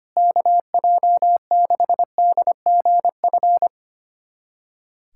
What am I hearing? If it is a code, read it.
Morse code, heard as a keyed tone, about 25 wpm, 700 Hz. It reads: KJ6DGF
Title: USA Callsign DE Callsign - 6 Characters - 25wpm